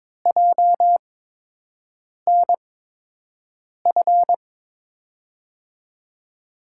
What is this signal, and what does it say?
Morse code, heard as a keyed tone, about 22 wpm, spaced as for 6 wpm, 700 Hz. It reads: JNF
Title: USA Call Signs - 6 Characters - 8x Character Spacing - 22wpm